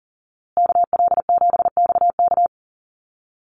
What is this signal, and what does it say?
Morse code, heard as a keyed tone, about 40 wpm, 700 Hz. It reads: KL7XK
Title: USA Call Signs Repeated 2x 40wpm